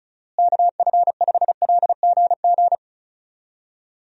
Morse code, heard as a keyed tone, about 34 wpm, 700 Hz. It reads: KF5LGG
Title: USA Call Signs - 6 Characters - Repeated 2x - 35wpm